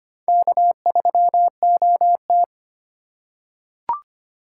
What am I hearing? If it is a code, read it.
Morse code, heard as a keyed tone, about 25 wpm, 700 Hz. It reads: K3OT
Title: USA Call Signs - 2x Character Spacing - 25wpm